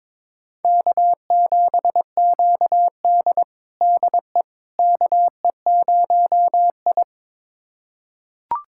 Morse code, heard as a keyed tone, about 22 wpm, 700 Hz. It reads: K7QD DE KE0I
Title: USA Callsign DE Callsign - 4 Characters - 22wpm